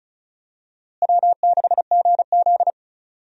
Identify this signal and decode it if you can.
Morse code, heard as a keyed tone, about 35 wpm, 700 Hz. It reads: W6GZ
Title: USA Call Signs - 4 Characters - Repeated 2x - 35wpm